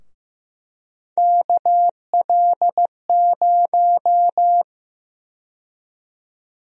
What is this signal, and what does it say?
Morse code, heard as a keyed tone, about 16 wpm, 700 Hz. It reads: KL0
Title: USA Call Sign Prefixes - Repeated 2x - 15wpm